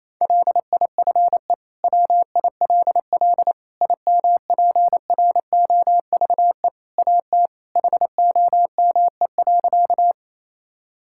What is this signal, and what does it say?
Morse code, heard as a keyed tone, about 28 wpm, 700 Hz. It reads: LIFE WILL IMPROVE AT HOME.
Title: Sentences from Top 1000 Words - Review Part 5 - 28wpm